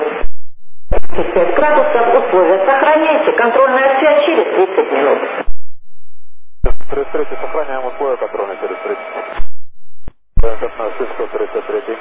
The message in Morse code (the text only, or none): none